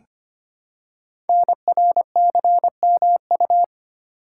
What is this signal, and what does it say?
Morse code, heard as a keyed tone, about 25 wpm, 700 Hz. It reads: NRCMU
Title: Code Groups - 5 Letters - 25wpm